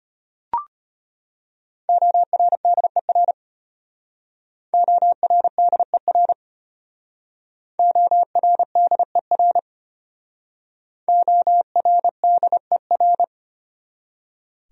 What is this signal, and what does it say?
Morse code, heard as a keyed tone, about 32 wpm, 700 Hz. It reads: ORDER ORDER ORDER ORDER